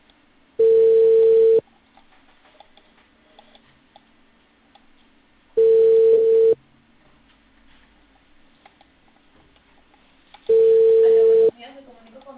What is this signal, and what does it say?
Ringback tone (China, 450 Hz), with speech.